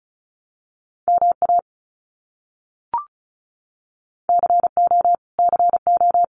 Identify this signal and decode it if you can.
Morse code, heard as a keyed tone, about 34 wpm, 700 Hz. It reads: MA CO CO
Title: US States Repeated 3x 35wpm